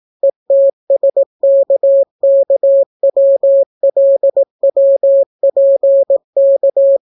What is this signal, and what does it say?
Morse code, heard as a keyed tone, about 18 wpm, 550 Hz. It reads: ETSKKWLWPK